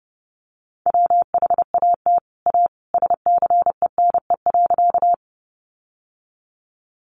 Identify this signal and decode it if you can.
Morse code, heard as a keyed tone, about 30 wpm, 700 Hz. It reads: WHAT A SCENE.
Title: Sentences from Top 1100 Words 30wpm